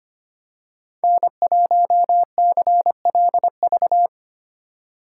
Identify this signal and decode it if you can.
Morse code, heard as a keyed tone, about 29 wpm, 700 Hz. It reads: N1CLV
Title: Speed Racing - USA 1x3 Call Signs 25wpm